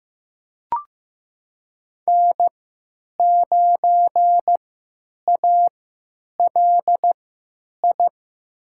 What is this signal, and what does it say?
Morse code, heard as a keyed tone, about 15 wpm, spaced as for 9 wpm, 700 Hz. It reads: N9ALI
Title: USA Call Signs - 3x Character Spacing - 15wpm